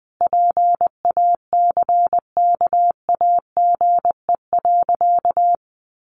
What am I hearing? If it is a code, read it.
Morse code, heard as a keyed tone, about 20 wpm, 700 Hz. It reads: PACKAGE.
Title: Additional Easy Phrases 20wpm